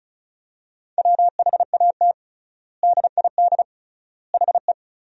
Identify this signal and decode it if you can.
Morse code, heard as a keyed tone, about 34 wpm, 700 Hz. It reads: WHAT DID HE
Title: 3x Word-Spacing - Sentences from Top 200 Words - 35wpm